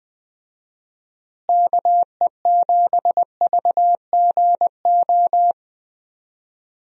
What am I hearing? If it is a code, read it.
Morse code, heard as a keyed tone, about 21 wpm, 700 Hz. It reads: KE7VGO
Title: USA Call Signs - 6 Characters - Repeated 3x - 20wpm